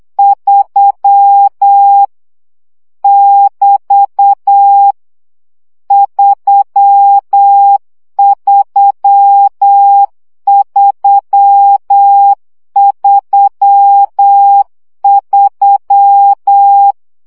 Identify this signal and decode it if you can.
Morse code, heard as a keyed tone, about 8 wpm, 800 Hz. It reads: 3 <BT> 33333
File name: CW gyakorlas 3 7 baud